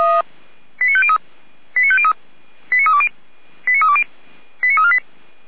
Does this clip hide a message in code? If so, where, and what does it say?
Yes: five-tone selective call (ZVEI) from 0.8 to 5.0 s, heard as quick runs of five tones, tones of about 70 ms. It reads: 89503 89503 89320 89320 89348